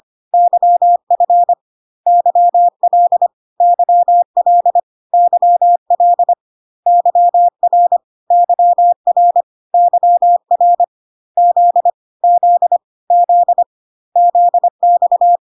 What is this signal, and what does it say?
Morse code, heard as a keyed tone, about 25 wpm, 700 Hz. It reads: YF YL YL YL YR YR YR Z Z Z ZX
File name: CW abbreviations alphabetical repeated 3 times 25 25wpm ik1hge